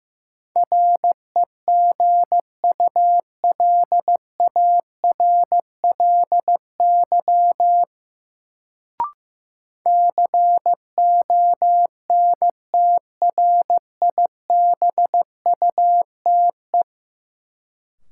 Morse code, heard as a keyed tone, about 15 wpm, 700 Hz. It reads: REGULARLY CONTRIBUTE
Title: Sentences from Top 1400 Words - New Word Drill Alt - 15wpm